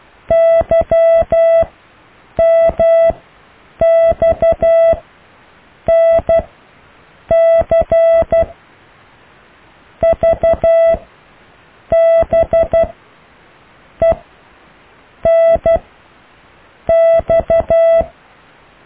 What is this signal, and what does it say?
Morse code, heard as a keyed tone, about 12 wpm, spaced as for 6 wpm, 660 Hz. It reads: YMXNC VBENX
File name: cw / Alle letters